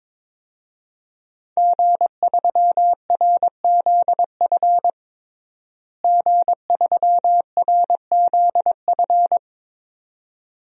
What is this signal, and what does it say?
Morse code, heard as a keyed tone, about 22 wpm, 700 Hz. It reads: G3RZF G3RZF
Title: DXCC-100 Call Signs - 5 Characters - Repeated 2x - 22wpm